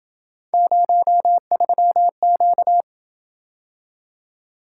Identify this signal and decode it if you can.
Morse code, heard as a keyed tone, about 27 wpm, 700 Hz. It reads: 03Q